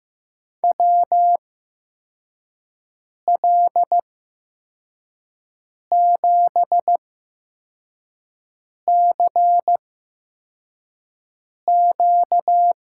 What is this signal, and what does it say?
Morse code, heard as a keyed tone, about 15 wpm, spaced as for 4 wpm, 700 Hz. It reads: WL7CQ
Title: USA Call Signs - 6 Characters - 8x Character Spacing - 15wpm